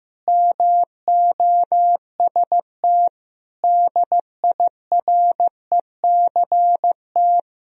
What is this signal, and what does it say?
Morse code, heard as a keyed tone, about 15 wpm, 700 Hz. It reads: MOST DIRECT
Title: Sentences from Top 1000 Words - Review Part 2 - 15wpm